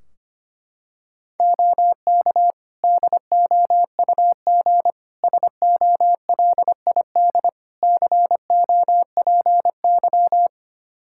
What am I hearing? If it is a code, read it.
Morse code, heard as a keyed tone, about 25 wpm, 700 Hz. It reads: OK DOUG SOLID COPY
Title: Practice QSO - Exchange 3 - More Info - 25wpm